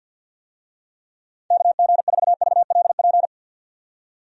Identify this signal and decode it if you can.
Morse code, heard as a keyed tone, about 49 wpm, 700 Hz. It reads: KG4VLP
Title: USA Call Signs - 6 Characters - 50wpm